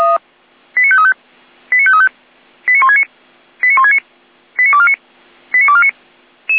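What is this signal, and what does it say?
Five-tone selective call (ZVEI), heard as quick runs of five tones, tones of about 80 ms. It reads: 89535 89535 89169 89169 89249 89249
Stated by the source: ZVEI five-tone selective call received on SDR.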